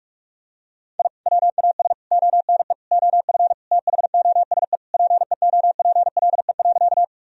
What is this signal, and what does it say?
Morse code, heard as a keyed tone, about 46 wpm, 700 Hz. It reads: I WAS ONE OF THOSE PEOPLE.
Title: Sentences from Top 1000 Words - Review Part 1 - 45wpm